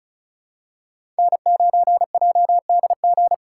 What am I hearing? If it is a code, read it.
Morse code, heard as a keyed tone, about 36 wpm, 700 Hz. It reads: N9JDG
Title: USA Call Signs - 5 Characters - 2x Character Spacing - 35wpm